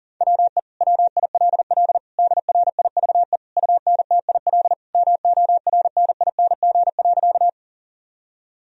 Morse code, heard as a keyed tone, about 40 wpm, 700 Hz. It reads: WE WILL DRIVE UNTIL MORNING.
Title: Sentences from Top 1000 Words - Review Part 2 - 40wpm